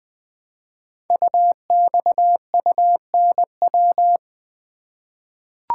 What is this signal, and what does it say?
Morse code, heard as a keyed tone, about 20 wpm, 700 Hz. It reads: UXUNW
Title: Code Groups - 5 Letters - 20wpm